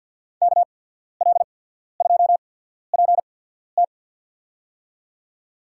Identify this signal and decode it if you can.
Morse code, heard as a keyed tone, about 53 wpm, spaced as for 14 wpm, 700 Hz. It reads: KL2PT
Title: USA Call Signs - 5 Characters - 8x Character Spacing - 50wpm